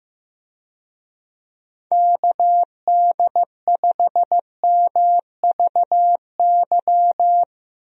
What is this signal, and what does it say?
Morse code, heard as a keyed tone, about 15 wpm, 700 Hz. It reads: KD5MVY
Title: USA Call Signs - 6 Characters - Repeated 2x - 15wpm